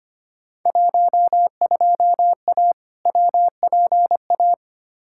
Morse code, heard as a keyed tone, about 25 wpm, 700 Hz. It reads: 12A WPA
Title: Field Day Exchanges Encore - 25wpm